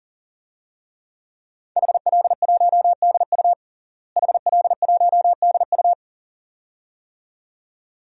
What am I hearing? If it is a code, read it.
Morse code, heard as a keyed tone, about 37 wpm, 700 Hz. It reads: HL1DU HL1DU
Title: DXCC-100 Call Signs - 5 Characters - Repeated 2x - 40wpm